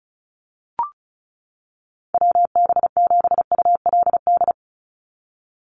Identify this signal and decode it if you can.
Morse code, heard as a keyed tone, about 34 wpm, 700 Hz. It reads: WB7ULD
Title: NA SOTA Chaser Callsigns 35wpm